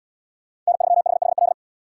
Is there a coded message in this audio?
Yes: Morse code from 0.7 to 1.5 s, heard as a keyed tone, about 72 wpm, 700 Hz. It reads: N3HHL